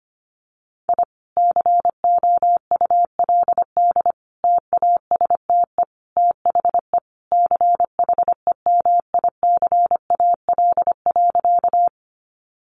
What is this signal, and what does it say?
Morse code, heard as a keyed tone, about 25 wpm, 700 Hz. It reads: I COULD TASTE THE CHEMICAL.
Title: Sentences from Top 1200 Words 25wpm